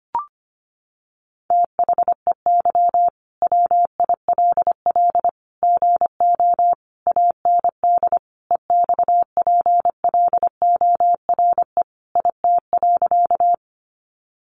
Morse code, heard as a keyed tone, about 25 wpm, 700 Hz. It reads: THEY WILL GO AND EXPLORE IT.
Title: Sentences from Top 1100 Words 25wpm